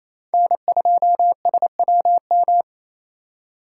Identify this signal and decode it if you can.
Morse code, heard as a keyed tone, about 27 wpm, 700 Hz. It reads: N2SWM
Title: USA Call Signs - 5 Characters - 4x Character Spacing - 28wpm